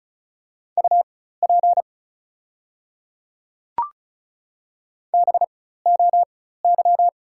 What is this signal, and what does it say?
Morse code, heard as a keyed tone, about 36 wpm, spaced as for 16 wpm, 700 Hz. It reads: UP BOY
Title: 4x Character Spacing - Top 200 Words - 35wpm